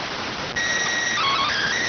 SSTV (slow-scan television), PD 290.